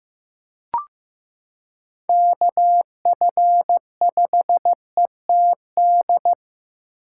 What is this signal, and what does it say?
Morse code, heard as a keyed tone, about 15 wpm, 700 Hz. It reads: KF5ETD
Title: K1USN SST QSO Practice - 15wpm